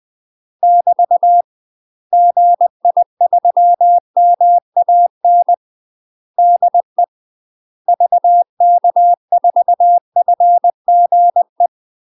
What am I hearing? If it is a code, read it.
Morse code, heard as a keyed tone, about 20 wpm, 700 Hz. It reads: <BT> GI3MAN DE VK4FGE
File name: UK MASTER QSOs TRAINING / 20WPM / UK MASTER QSOs 0002